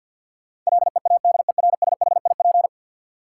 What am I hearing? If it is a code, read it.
Morse code, heard as a keyed tone, about 51 wpm, 700 Hz. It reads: LEADERSHIP
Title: Sentences from Top 1600 Words - New Word Drill - 50wpm